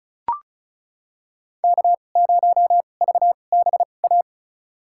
Morse code, heard as a keyed tone, about 34 wpm, spaced as for 25 wpm, 700 Hz. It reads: K0VBA
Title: USA Call Signs - 5 Characters - 2x Character Spacing - 35wpm